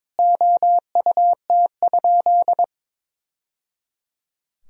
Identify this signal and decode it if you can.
Morse code, heard as a keyed tone, about 21 wpm, 700 Hz. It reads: OUT?
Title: Sentences from Top 1000 Words - Review Part 2 - 22wpm